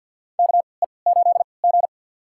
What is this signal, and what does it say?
Morse code, heard as a keyed tone, about 50 wpm, spaced as for 28 wpm, 700 Hz. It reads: KE8G
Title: USA Call Signs - 4 Characters - 3x Character Spacing - 50wpm